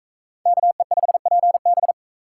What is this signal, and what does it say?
Morse code, heard as a keyed tone, about 44 wpm, 700 Hz. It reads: KE5PB